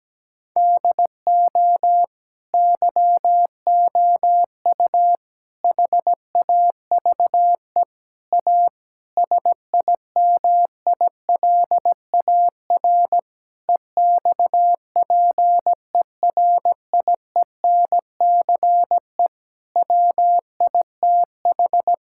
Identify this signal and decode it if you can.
Morse code, heard as a keyed tone, about 17 wpm, 700 Hz. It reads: DO YOU HAVE A SIMILAR EXPERIENCE WITH
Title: Sentences from Top 1000 Words - Review Part 5 - 17wpm